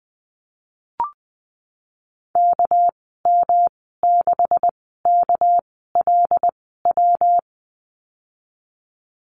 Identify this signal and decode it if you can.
Morse code, heard as a keyed tone, about 20 wpm, spaced as for 15 wpm, 700 Hz. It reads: KM6KLW
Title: 219 Lesson 35g - Number 0 - Call Signs 20wpm